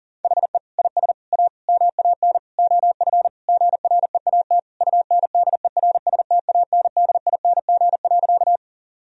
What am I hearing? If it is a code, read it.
Morse code, heard as a keyed tone, about 40 wpm, 700 Hz. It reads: HE IS A MAN OF GREAT UNDERSTANDING.
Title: Sentences from Top 1000 Words - Review Part 3 - 40wpm